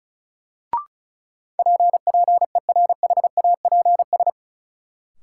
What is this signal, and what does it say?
Morse code, heard as a keyed tone, about 34 wpm, 700 Hz. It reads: PPERHAPS
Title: Sentences from Top 1100 Words - New Word Drill - 35wpm